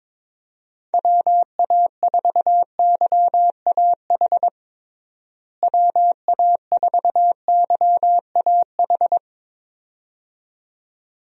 Morse code, heard as a keyed tone, about 22 wpm, 700 Hz. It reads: WA4YAH WA4YAH
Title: USA Call Signs - 6 Characters - Repeated 2x - 22wpm